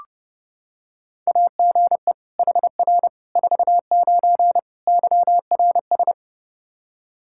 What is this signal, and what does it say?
Morse code, heard as a keyed tone, about 28 wpm, 700 Hz. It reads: AGE HR 49 YRS
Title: K1USN SST QSO Practice 30wpm